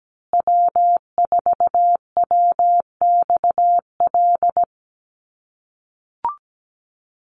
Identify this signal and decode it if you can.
Morse code, heard as a keyed tone, about 17 wpm, 700 Hz. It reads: W4WXL
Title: USA Call Signs Repeated 2x 17wpm